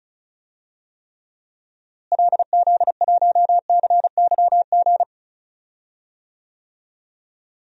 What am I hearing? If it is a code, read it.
Morse code, heard as a keyed tone, about 36 wpm, 700 Hz. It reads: LZ1CYG
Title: DXCC-100 Call Signs - 6 Characters - Repeated 2x - 35wpm